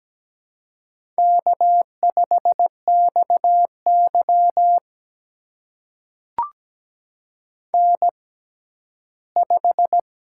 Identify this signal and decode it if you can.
Morse code, heard as a keyed tone, about 17 wpm, spaced as for 12 wpm, 700 Hz. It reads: K5XY N5
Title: USA Call Signs - 4 Characters - 6x Character Spacing - 17wpm